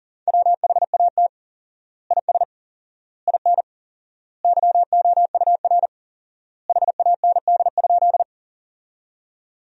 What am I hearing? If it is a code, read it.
Morse code, heard as a keyed tone, about 37 wpm, 700 Hz. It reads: WHAT IS IN YOUR HAND?
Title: 4x Word-Spacing - Sentences from Top 200 Words - 40wpm